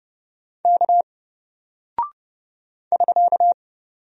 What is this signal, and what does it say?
Morse code, heard as a keyed tone, about 30 wpm, 700 Hz. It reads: K <SK>
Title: Single Character (Letter:Number:Punctuation:Prosign) Alt 30wpm